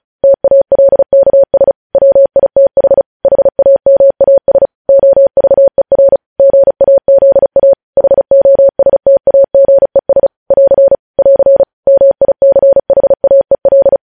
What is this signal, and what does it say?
Morse code, heard as a keyed tone, about 36 wpm, 550 Hz. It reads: TALKS WITH HAMAS OVER GAZA HOSTAGES <AR> <AR> MICHAEL
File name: podcast / News headlines-1x-35x35